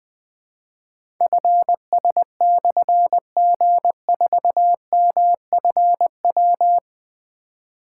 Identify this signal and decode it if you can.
Morse code, heard as a keyed tone, about 20 wpm, 700 Hz. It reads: FS/G4MFW
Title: Difficult Call Signs 20wpm